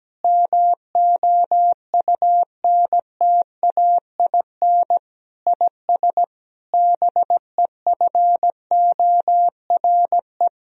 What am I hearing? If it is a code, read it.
Morse code, heard as a keyed tone, about 17 wpm, 700 Hz. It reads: MOUNTAIN IS BEFORE
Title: Sentences from Top 1000 Words - Review Part 1 - 17wpm